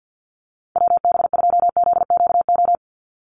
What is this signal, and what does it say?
Morse code, heard as a keyed tone, about 50 wpm, 700 Hz. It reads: WB2ZQO
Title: USA Call Signs 50wpm